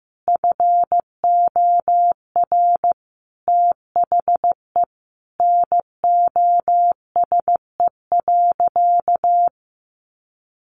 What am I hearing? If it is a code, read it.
Morse code, heard as a keyed tone, about 14 wpm, 700 Hz. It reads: FOR THE NOSE.
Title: Easy Phrases 15wpm